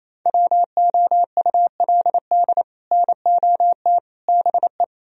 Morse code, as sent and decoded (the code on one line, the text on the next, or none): .-- --- ..- .-.. -.. / -. --- - / -... .
WOULD NOT BE